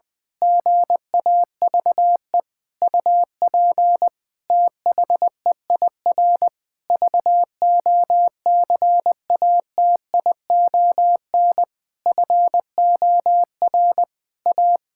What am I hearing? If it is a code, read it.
Morse code, heard as a keyed tone, about 20 wpm, 700 Hz. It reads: GAVE UP THEIR VOCATION FOR A
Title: Pangram Challenge v2 - 20wpm